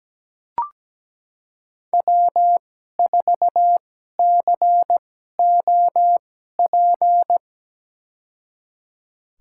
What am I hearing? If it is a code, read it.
Morse code, heard as a keyed tone, about 17 wpm, spaced as for 12 wpm, 700 Hz. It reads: W4COP
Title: USA Call Signs - 5 Characters - 2x Character Spacing - 17wpm